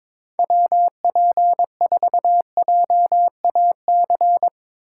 Morse code, heard as a keyed tone, about 22 wpm, 700 Hz. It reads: WP4JAC
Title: USA Call Signs - 6 Characters - 8x Character Spacing - 22wpm